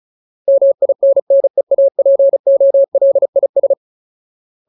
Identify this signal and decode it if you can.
Morse code, heard as a keyed tone, about 34 wpm, 550 Hz. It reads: MINNEAPOLIS
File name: Ultimate Word List / 35WPM Ultimate